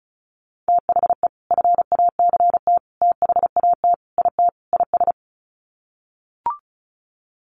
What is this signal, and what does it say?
Morse code, heard as a keyed tone, about 34 wpm, 700 Hz. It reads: THE FACT THAT IT IS
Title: Sets of 5 Words Encore2 35wpm